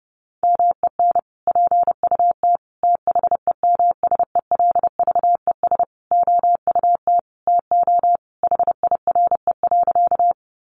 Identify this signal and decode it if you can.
Morse code, heard as a keyed tone, about 30 wpm, 700 Hz. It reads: MEN PUT THEMSELVES OUT TO HIRE.
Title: Sentences from Top 1400 Words 30wpm